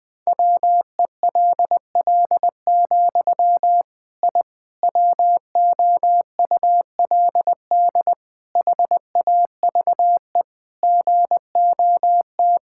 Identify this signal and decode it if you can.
Morse code, heard as a keyed tone, about 20 wpm, 700 Hz. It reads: WELL, I WOULD HAVE GOT
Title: Sentences from Top 1000 Words - Review Part 5 - 20wpm